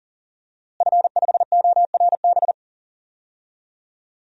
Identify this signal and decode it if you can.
Morse code, heard as a keyed tone, about 40 wpm, 700 Hz. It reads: F5ORB
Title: DXCC-100 Call Signs - 5 Characters - Repeated 2x - 40wpm